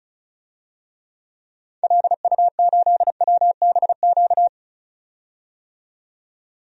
Morse code, heard as a keyed tone, about 36 wpm, 700 Hz. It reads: LU8WBQ
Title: DXCC-100 Call Signs - 6 Characters - 35wpm